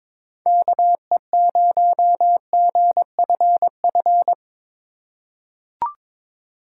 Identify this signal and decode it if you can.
Morse code, heard as a keyed tone, about 21 wpm, 700 Hz. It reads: KE0GFF
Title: USA Call Signs - 4x Character Spacing - 22wpm